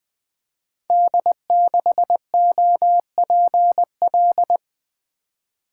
Morse code, heard as a keyed tone, about 20 wpm, 700 Hz. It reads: D6OPL
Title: Code Groups - 5 Letter:Numbers - 20wpm